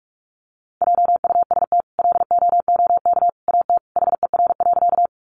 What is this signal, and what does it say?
Morse code, heard as a keyed tone, about 44 wpm, 700 Hz. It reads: JUST LOOK AT HER.
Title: Sentences from Top 200 Words 45wpm